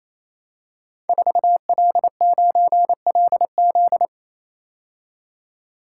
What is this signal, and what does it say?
Morse code, heard as a keyed tone, about 27 wpm, 700 Hz. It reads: 4L9LZ
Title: Code Groups - 5 Letter:Numbers - 28wpm